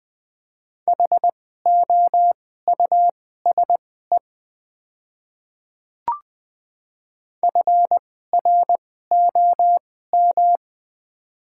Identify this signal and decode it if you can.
Morse code, heard as a keyed tone, about 20 wpm, spaced as for 14 wpm, 700 Hz. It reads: HOUSE FROM
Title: 2x Character Spacing - Top 200 Words - 20wpm